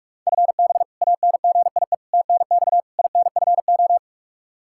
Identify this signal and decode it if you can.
Morse code, heard as a keyed tone, about 41 wpm, 700 Hz. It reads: FB ANGIE TNX INFO
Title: K1USN SST QSO Practice 45wpm